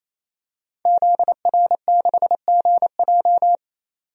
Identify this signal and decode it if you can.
Morse code, heard as a keyed tone, about 28 wpm, 700 Hz. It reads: ZR6GJ
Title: DXCC-100 Call Signs - 5 Characters - 28wpm